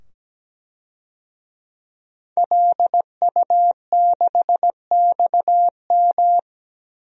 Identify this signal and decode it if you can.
Morse code, heard as a keyed tone, about 17 wpm, 700 Hz. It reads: LU6XM
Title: DXCC-100 Call Signs - 5 Characters - 17wpm